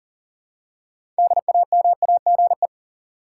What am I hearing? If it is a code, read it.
Morse code, heard as a keyed tone, about 40 wpm, 700 Hz. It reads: DAMAGE
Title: Six Letter Words - 40wpm